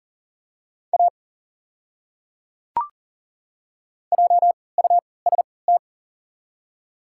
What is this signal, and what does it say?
Morse code, heard as a keyed tone, about 37 wpm, spaced as for 22 wpm, 700 Hz. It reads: A JUST
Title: 3x Character Spacing - Top 300 Words - 40wpm